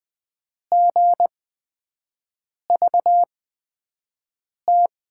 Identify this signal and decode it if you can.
Morse code, heard as a keyed tone, about 20 wpm, spaced as for 5 wpm, 700 Hz. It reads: GVT
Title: USA Call Signs - 6 Characters - 8x Character Spacing - 20wpm